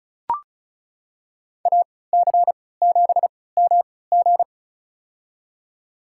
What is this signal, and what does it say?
Morse code, heard as a keyed tone, about 36 wpm, spaced as for 20 wpm, 700 Hz. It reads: AC7MG
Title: USA Call Signs - 5 Characters - 3x Character Spacing - 35wpm